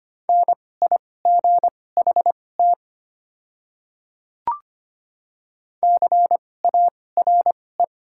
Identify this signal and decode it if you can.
Morse code, heard as a keyed tone, about 25 wpm, spaced as for 18 wpm, 700 Hz. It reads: NIGHT CARE
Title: 2x Character Spacing - Top 500 Words - 25wpm